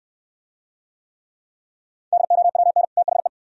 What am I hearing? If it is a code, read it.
Morse code, heard as a keyed tone, about 62 wpm, 700 Hz. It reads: DOWN THE